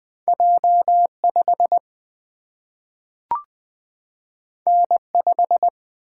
Morse code, heard as a keyed tone, about 20 wpm, 700 Hz. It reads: J 5 N 5
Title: Single-Letter Single-Number 20wpm